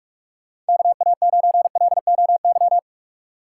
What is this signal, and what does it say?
Morse code, heard as a keyed tone, about 47 wpm, 700 Hz. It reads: KA9LOY
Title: USA Call Signs - 2x Character Spacing - 45wpm